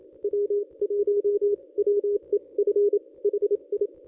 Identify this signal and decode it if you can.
Morse code, heard as a keyed tone, about 29 wpm, spaced as for 24 wpm, 410 Hz. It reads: W1WEF HI
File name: contests / ss-cw-k5tr-op-n5rz -1000